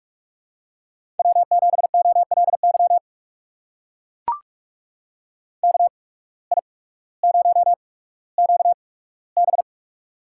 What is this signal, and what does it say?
Morse code, heard as a keyed tone, about 44 wpm, spaced as for 19 wpm, 700 Hz. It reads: W7OLY KI0QB
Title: USA Call Signs - 8x Character Spacing - 45wpm